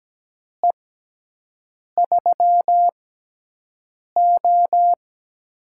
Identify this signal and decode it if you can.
Morse code, heard as a keyed tone, about 17 wpm, spaced as for 6 wpm, 700 Hz. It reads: E3O
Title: USA Call Signs - 4 Characters - 6x Character Spacing - 17wpm